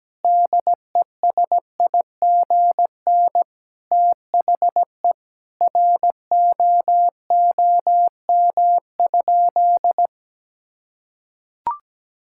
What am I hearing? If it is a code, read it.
Morse code, heard as a keyed tone, about 17 wpm, 700 Hz. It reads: DESIGN THE ROOM?
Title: Sentences from Top 1000 Words - Review Part 5 - 17wpm